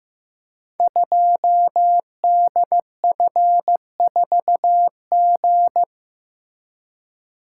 Morse code, heard as a keyed tone, about 15 wpm, 700 Hz. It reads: 2DF4G
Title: Code Groups - 5 Letter:Numbers - 15wpm